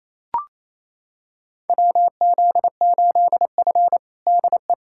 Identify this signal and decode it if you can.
Morse code, heard as a keyed tone, about 28 wpm, 700 Hz. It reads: WZ8F DE
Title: USA Callsign DE Callsign - 4 Characters - 28wpm